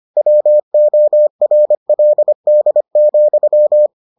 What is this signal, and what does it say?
Morse code, heard as a keyed tone, about 25 wpm, 600 Hz. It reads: WORLD,